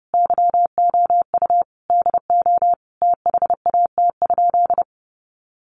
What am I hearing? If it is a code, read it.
Morse code, heard as a keyed tone, about 30 wpm, 700 Hz. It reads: YOU DO THAT?
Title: Sentences from Top 100 Words 30wpm